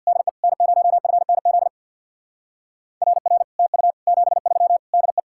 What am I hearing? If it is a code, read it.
Morse code, heard as a keyed tone, about 58 wpm, 700 Hz. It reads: DE N0FNZ RR TU 73 DE
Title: Speed-Racing Common QSO Phrases - 50wpm